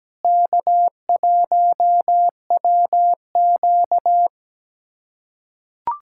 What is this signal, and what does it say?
Morse code, heard as a keyed tone, about 17 wpm, 700 Hz. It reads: K1WQ
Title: USA Callsign DE Callsign - 4 Characters - 17wpm